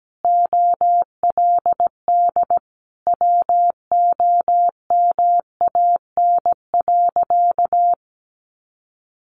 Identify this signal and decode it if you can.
Morse code, heard as a keyed tone, about 17 wpm, 700 Hz. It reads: OLD WOMAN.
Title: Sentences from Top 1100 Words 17wpm